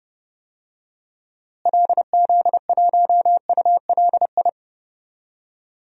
Morse code, heard as a keyed tone, about 30 wpm, 700 Hz. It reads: LZ1ULI
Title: DXCC-100 Call Signs - 6 Characters - Repeated 2x - 30wpm